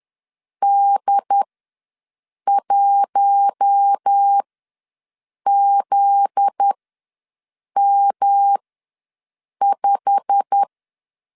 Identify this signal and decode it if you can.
Morse code, heard as a keyed tone, about 10 wpm, spaced as for 6 wpm, 800 Hz. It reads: D1ZM5
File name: LEZIONI / 8 YJ45 / 8 YJ45 ws 10wpm 3843